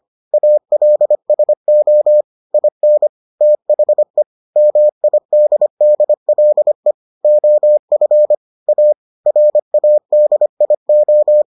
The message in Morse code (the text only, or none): ALSO IN THE MIDDLE OF A RADIO